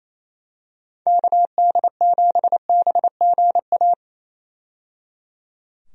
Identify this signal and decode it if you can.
Morse code, heard as a keyed tone, about 27 wpm, 700 Hz. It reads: KD7BGA
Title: USA Call Signs - 6 Characters - 28wpm